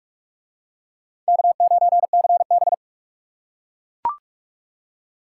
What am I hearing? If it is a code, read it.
Morse code, heard as a keyed tone, about 40 wpm, 700 Hz. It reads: K9CB